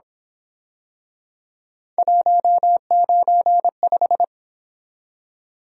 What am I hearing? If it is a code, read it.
Morse code, heard as a keyed tone, about 27 wpm, 700 Hz. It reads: 195
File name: Speed-Racing - Three Letter-Number Combinations - 22wpm